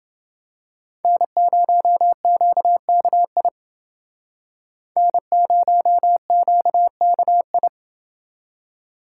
Morse code, heard as a keyed tone, about 28 wpm, 700 Hz. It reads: N0QKI N0QKI